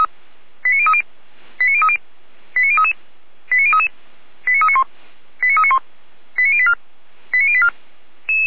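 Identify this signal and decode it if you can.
Five-tone selective call (ZVEI), heard as quick runs of five tones, tones of about 70 ms. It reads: 89030 89030 89033 89033 89391 89391 89084 89084
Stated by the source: ZVEI five-tone selective call received by SDR.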